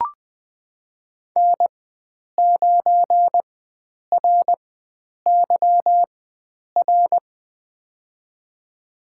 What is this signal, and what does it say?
Morse code, heard as a keyed tone, about 20 wpm, spaced as for 9 wpm, 700 Hz. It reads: N9RYR
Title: USA Call Signs - 5 Characters - 4x Character Spacing - 20wpm